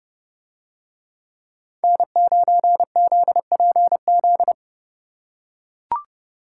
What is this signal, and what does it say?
Morse code, heard as a keyed tone, about 29 wpm, 700 Hz. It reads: N9ZPZ